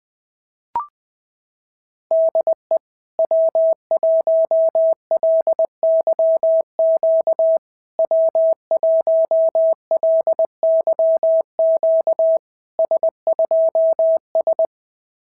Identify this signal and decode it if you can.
Morse code, heard as a keyed tone, about 20 wpm, 650 Hz. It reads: DE W1LYQ W1LYQ S2S
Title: SOTA QSO Practice 20wpm